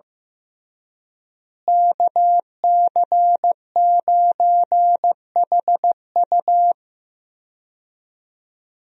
Morse code, heard as a keyed tone, about 15 wpm, 700 Hz. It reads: KC9HU
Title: USA Call Signs - 5 Characters - Repeated 3x - 15wpm